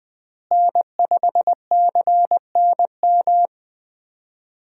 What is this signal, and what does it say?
Morse code, heard as a keyed tone, about 20 wpm, 700 Hz. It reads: N5CNM
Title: USA Call Signs - 5 Characters - Repeated 2x - 20wpm